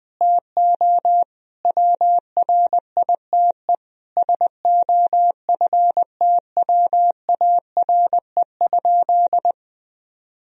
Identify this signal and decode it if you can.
Morse code, heard as a keyed tone, about 20 wpm, 700 Hz. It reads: TO WRITE SOFTWARE?
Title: Sentences from Top 1000 Words - Review Part 3 - 20wpm